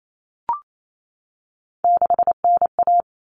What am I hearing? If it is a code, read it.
Morse code, heard as a keyed tone, about 28 wpm, 700 Hz. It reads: 6NA
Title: Common Cut Numbers - 3-Digit 28wpm